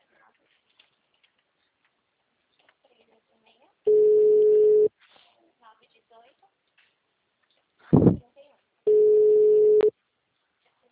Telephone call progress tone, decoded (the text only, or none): ringback tone